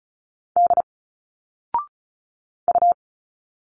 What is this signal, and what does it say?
Morse code, heard as a keyed tone, about 35 wpm, 700 Hz. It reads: D U
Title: Single Letters 35wpm